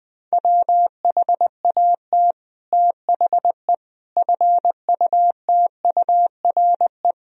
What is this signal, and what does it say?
Morse code, heard as a keyed tone, about 20 wpm, 700 Hz. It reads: WHAT THE FUTURE
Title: Sentences from Top 1000 Words - Review Part 2 - 20wpm